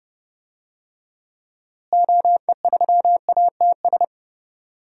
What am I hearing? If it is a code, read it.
Morse code, heard as a keyed tone, about 30 wpm, 700 Hz. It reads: OE3ATS
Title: DXCC-100 Call Signs - 6 Characters - Repeated 3x - 30wpm